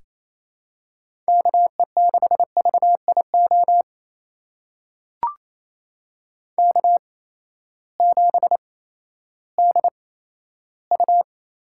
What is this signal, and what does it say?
Morse code, heard as a keyed tone, about 28 wpm, spaced as for 14 wpm, 700 Hz. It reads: KE6VIO K7DU
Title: USA Call Signs - 8x Character Spacing - 28wpm